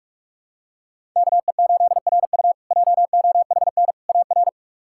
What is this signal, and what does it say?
Morse code, heard as a keyed tone, about 42 wpm, 700 Hz. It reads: KE8RU JOHN AR
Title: K1USN SST QSO Practice - 45wpm